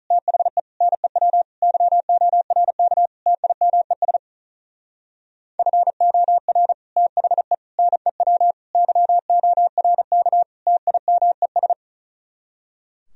Morse code, heard as a keyed tone, about 44 wpm, 700 Hz. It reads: THE NEW YORK TIMES FOR THE NEW YORK TIMES